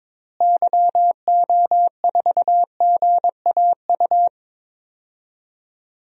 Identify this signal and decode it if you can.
Morse code, heard as a keyed tone, about 22 wpm, 700 Hz. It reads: YO4GAU
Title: DXCC-100 Call Signs - 6 Characters - Repeated 3x - 22wpm